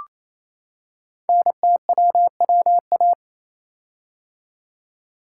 Morse code, heard as a keyed tone, about 28 wpm, 700 Hz. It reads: NTWWA
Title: Code Groups - 5 Letters - 28wpm